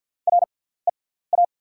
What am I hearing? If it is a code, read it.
Morse code, heard as a keyed tone, about 48 wpm, spaced as for 17 wpm, 700 Hz. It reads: REA
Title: 6x Character Spacing - Top 200 Words - 50wpm